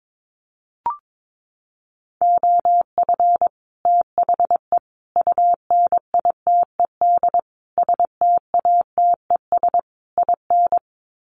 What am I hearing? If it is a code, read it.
Morse code, heard as a keyed tone, about 22 wpm, 700 Hz. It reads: OF THE UNITED STATES IN
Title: Sets of 5 Words Encore 22wpm